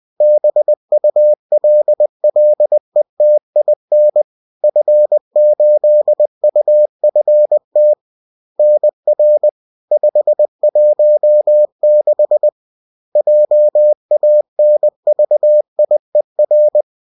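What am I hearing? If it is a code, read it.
Morse code, heard as a keyed tone, about 20 wpm, 600 Hz. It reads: BULLETIN F8UFT NR 516 JANVIER
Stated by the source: Bulletin sent in CW.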